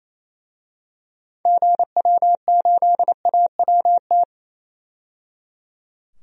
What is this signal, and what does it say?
Morse code, heard as a keyed tone, about 27 wpm, 700 Hz. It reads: GW8AWT
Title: DXCC-100 Call Signs - 6 Characters - 28wpm